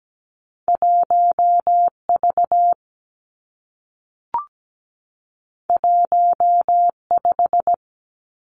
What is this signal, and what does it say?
Morse code, heard as a keyed tone, about 17 wpm, 700 Hz. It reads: 1V 15
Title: Two Challenging Letter-Number Combinations 17wpm